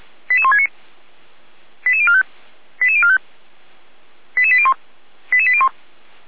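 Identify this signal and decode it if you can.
Five-tone selective call (ZVEI), heard as quick runs of five tones, tones of about 70 ms. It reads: 88169 88045 88045 88801 88801
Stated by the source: ZVEI five-tone selective call received by SDR.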